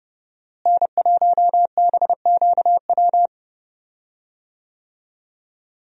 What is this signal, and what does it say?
Morse code, heard as a keyed tone, about 30 wpm, 700 Hz. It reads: N1BQW
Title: USA Call Signs - 5 Characters - Repeated 2x - 30wpm